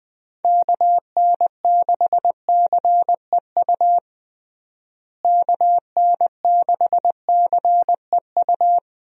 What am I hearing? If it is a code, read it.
Morse code, heard as a keyed tone, about 20 wpm, 700 Hz. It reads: KN6CEU KN6CEU
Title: USA Call Signs - 6 Characters - Repeated 2x - 20wpm